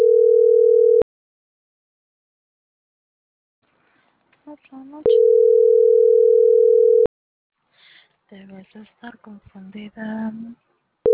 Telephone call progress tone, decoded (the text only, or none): ringback tone